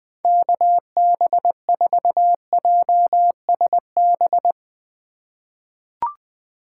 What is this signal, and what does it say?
Morse code, heard as a keyed tone, about 20 wpm, 700 Hz. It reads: KB4JSB
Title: USA Call Signs - 6 Characters - 3x Character Spacing - 20wpm